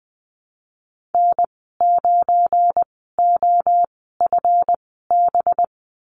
Morse code, heard as a keyed tone, about 20 wpm, spaced as for 15 wpm, 700 Hz. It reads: N9OFB
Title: 193 Lesson 31g - Letter v - Call Signs 20wpm